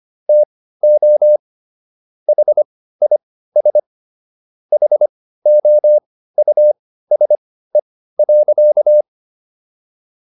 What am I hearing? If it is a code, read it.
Morse code, heard as a keyed tone, about 25 wpm, spaced as for 15 wpm, 600 Hz. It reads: TO HIS HOUSE.